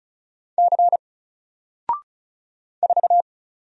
Morse code, heard as a keyed tone, about 36 wpm, 700 Hz. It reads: C 4
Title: Single Letter-Number Alt 35wpm